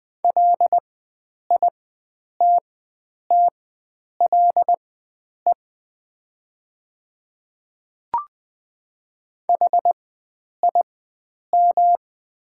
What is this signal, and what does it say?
Morse code, heard as a keyed tone, about 20 wpm, spaced as for 9 wpm, 700 Hz. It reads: LITTLE HIM
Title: 4x Character Spacing - Top 300 Words - 20wpm